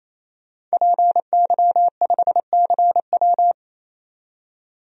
Morse code, heard as a keyed tone, about 28 wpm, 700 Hz. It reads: PY5CW
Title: DXCC-100 Call Signs - 5 Characters - 28wpm